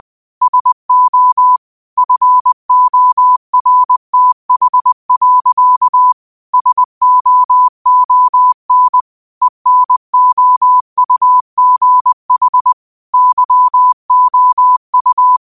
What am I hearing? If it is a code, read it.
Morse code, heard as a keyed tone, about 20 wpm, 1000 Hz. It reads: SO FORTH. SOON ENOUGH YOU